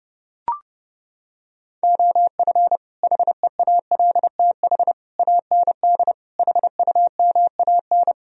Morse code, heard as a keyed tone, about 30 wpm, 700 Hz. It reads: OF HEALTH AND HUMAN